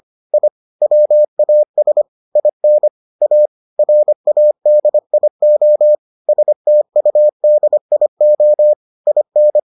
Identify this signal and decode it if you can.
Morse code, heard as a keyed tone, about 25 wpm, 600 Hz. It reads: I WAS IN A RADIO STUDIO IN